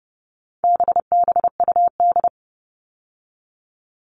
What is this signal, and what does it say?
Morse code, heard as a keyed tone, about 30 wpm, 700 Hz. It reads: BBUD
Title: All Cut Numbers - 4-Digit 30wpm